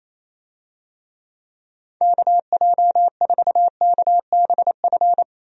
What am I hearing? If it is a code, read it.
Morse code, heard as a keyed tone, about 28 wpm, 700 Hz. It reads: KJ4KBF
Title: USA Call Signs - 6 Characters - 28wpm